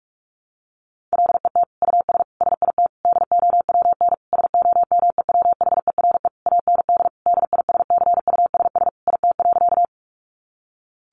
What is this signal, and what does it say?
Morse code, heard as a keyed tone, about 44 wpm, 700 Hz. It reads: LET US SIT DOWN SOMEWHERE AND DISCUSS IT.
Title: Sentences from Top 1400 Words 45wpm